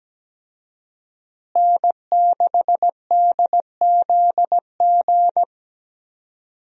Morse code, heard as a keyed tone, about 17 wpm, 700 Hz. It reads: N6DZG
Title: USA Call Signs - 5 Characters - Repeated 3x - 17wpm